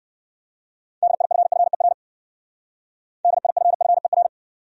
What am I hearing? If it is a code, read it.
Morse code, heard as a keyed tone, about 61 wpm, 700 Hz. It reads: DIFFER DIFFER